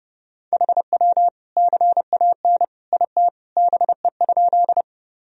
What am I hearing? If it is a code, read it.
Morse code, heard as a keyed tone, about 30 wpm, 700 Hz. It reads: HW CAN IT BE?
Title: Sentences from Top 100 Words and Abbreviations 30wpm